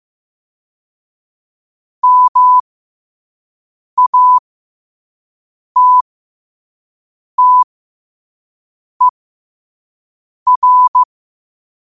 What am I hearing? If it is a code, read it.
Morse code, heard as a keyed tone, about 15 wpm, spaced as for 5 wpm, 1000 Hz. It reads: MATTER
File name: 2026-02-15-5wpm